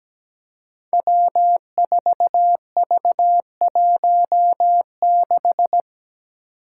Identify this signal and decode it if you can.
Morse code, heard as a keyed tone, about 17 wpm, 700 Hz. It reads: W4V16
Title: Five Challenging Letter-Number Combinations 17wpm